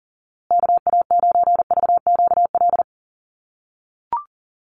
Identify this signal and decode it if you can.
Morse code, heard as a keyed tone, about 40 wpm, 700 Hz. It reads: KA9VQL
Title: USA Call Signs Repeated 3x 40wpm